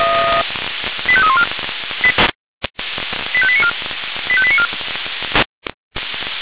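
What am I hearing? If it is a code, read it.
Five-tone selective call (ZVEI), heard as quick runs of five tones, tones of about 70 ms. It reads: 85314 85894 85894
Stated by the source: ZVEI five-tone selective call received on an SDR.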